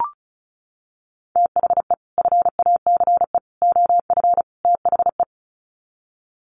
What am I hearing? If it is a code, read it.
Morse code, heard as a keyed tone, about 34 wpm, 700 Hz. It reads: THE FACE OF THE
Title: Sets of 4 Words 35wpm